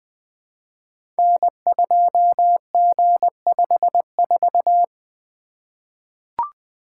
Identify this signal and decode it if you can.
Morse code, heard as a keyed tone, about 20 wpm, 700 Hz. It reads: N2G54
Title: Code Groups - 5 Letter:Numbers - 20wpm